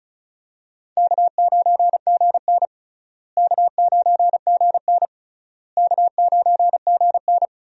Morse code, heard as a keyed tone, about 36 wpm, 700 Hz. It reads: K9GN K9GN K9GN
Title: USA Call Signs - 4 Characters - Repeated 3x - 35wpm